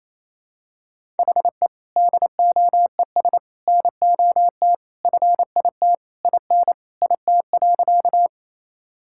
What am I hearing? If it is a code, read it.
Morse code, heard as a keyed tone, about 28 wpm, 700 Hz. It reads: HE DOES NOT FIT IN IT.
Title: Sentences from Top 1000 Words - Review Part 5 - 28wpm